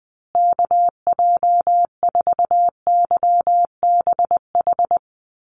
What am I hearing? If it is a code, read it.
Morse code, heard as a keyed tone, about 20 wpm, 700 Hz. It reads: KJ4YBH
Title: USA Call Signs Repeated 2x 20wpm